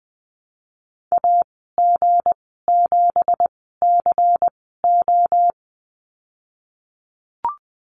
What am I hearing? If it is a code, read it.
Morse code, heard as a keyed tone, about 19 wpm, spaced as for 14 wpm, 700 Hz. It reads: AG7CO
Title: 186 Lesson 30g - Letter b - Call Signs 20wpm